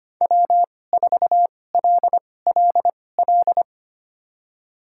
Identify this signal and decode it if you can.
Morse code, heard as a keyed tone, about 25 wpm, spaced as for 18 wpm, 700 Hz. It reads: W4LLL
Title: USA Call Signs - 5 Characters - 2x Character Spacing - 25wpm